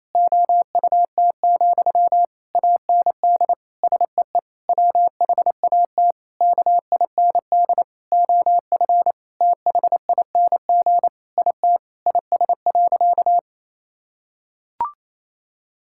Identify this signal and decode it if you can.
Morse code, heard as a keyed tone, about 27 wpm, 700 Hz. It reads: OUT, AND SEE WHAT KIND OF THING IT IS.
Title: Sentences from Top 1000 Words - Review Part 6 - 28wpm